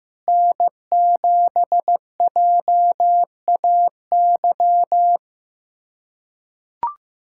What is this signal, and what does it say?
Morse code, heard as a keyed tone, about 15 wpm, 700 Hz. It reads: N7JAY
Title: USA Callsign DE Callsign - 5 Characters - 15wpm